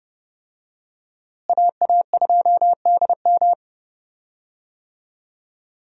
Morse code, heard as a keyed tone, about 30 wpm, 700 Hz. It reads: AA2DM
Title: USA Call Signs - 5 Characters - Repeated 2x - 30wpm